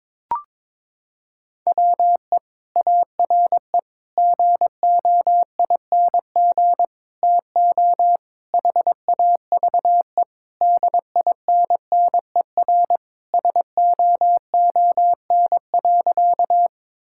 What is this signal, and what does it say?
Morse code, heard as a keyed tone, about 21 wpm, 700 Hz. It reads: WE ARE GOING TO HAVE DINNER SOON.
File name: practice / 22wpm / Sentences with 3-Word Phrases - 22wpm